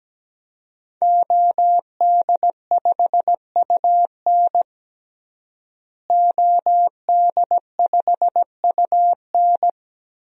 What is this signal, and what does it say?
Morse code, heard as a keyed tone, about 17 wpm, 700 Hz. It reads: OD5UN OD5UN
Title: DXCC-100 Call Signs - 5 Characters - Repeated 3x - 17wpm